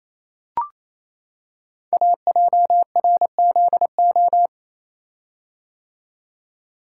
Morse code, heard as a keyed tone, about 29 wpm, 700 Hz. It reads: AJRZO
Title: Code Groups - 5 Letter:Numbers - 28wpm